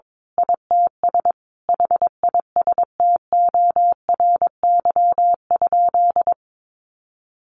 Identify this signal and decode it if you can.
Morse code, heard as a keyed tone, about 22 wpm, 700 Hz. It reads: ITS HISTORY?
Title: Sentences from Top 1100 Words 22wpm